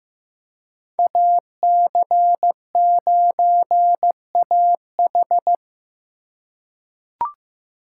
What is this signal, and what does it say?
Morse code, heard as a keyed tone, about 15 wpm, 700 Hz. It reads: AC9AH
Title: USA Call Signs - 5 Characters - 3x Character Spacing - 15wpm